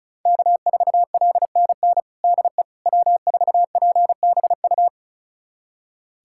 Morse code, heard as a keyed tone, about 36 wpm, 700 Hz. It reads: K4LNN DE W4PBU
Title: USA Callsign DE Callsign - 5 Characters - 35wpm